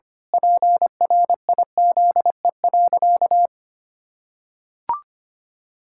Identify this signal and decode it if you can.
Morse code, heard as a keyed tone, about 25 wpm, 700 Hz. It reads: PRIZE.
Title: Pangram Challenge v2 - 25wpm